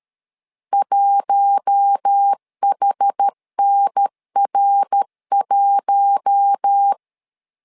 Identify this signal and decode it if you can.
Morse code, heard as a keyed tone, about 13 wpm, 800 Hz. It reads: 1HNR1
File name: LEZIONI / 4 Q1 12wpm 2028